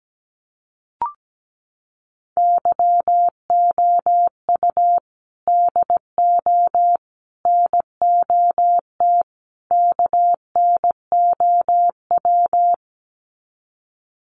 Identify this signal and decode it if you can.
Morse code, heard as a keyed tone, about 17 wpm, 700 Hz. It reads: YOU DO NOT KNOW
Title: Sets of 4 Words 17wpm